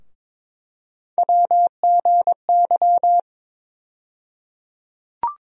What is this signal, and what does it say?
Morse code, heard as a keyed tone, about 22 wpm, 700 Hz. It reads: WGY
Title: USA Call Sign Postfixes - 3 Letters 22wpm